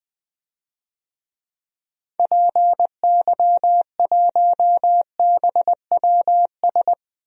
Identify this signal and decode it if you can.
Morse code, heard as a keyed tone, about 20 wpm, 700 Hz. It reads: PY1BWS
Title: DXCC-100 Call Signs - 6 Characters - 20wpm